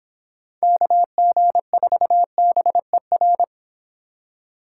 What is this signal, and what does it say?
Morse code, heard as a keyed tone, about 29 wpm, 700 Hz. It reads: KG4BER